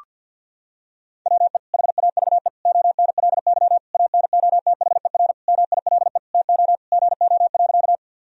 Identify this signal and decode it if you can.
Morse code, heard as a keyed tone, about 50 wpm, 700 Hz. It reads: WE HAVE ONLY ANOTHER MILE TO GO.
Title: Sentences from Top 1000 Words - Review Part 1 - 50wpm